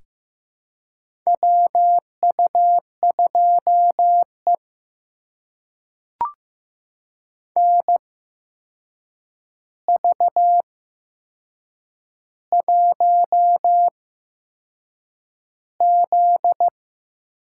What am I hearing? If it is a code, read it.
Morse code, heard as a keyed tone, about 15 wpm, spaced as for 6 wpm, 700 Hz. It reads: WU2E NV1Z
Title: USA Call Signs - 4 Characters - 8x Character Spacing - 15wpm